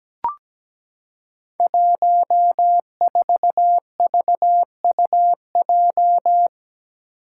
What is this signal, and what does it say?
Morse code, heard as a keyed tone, about 17 wpm, 700 Hz. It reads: 14VUJ
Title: Five Challenging Letter-Number Combinations 17wpm